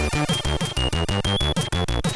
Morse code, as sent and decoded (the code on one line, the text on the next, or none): none